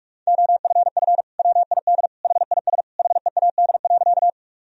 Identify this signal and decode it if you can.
Morse code, heard as a keyed tone, about 46 wpm, 700 Hz. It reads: OFF WID HIS HEAD.
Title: Sentences from Top 300 Words and Abbreviations 45wpm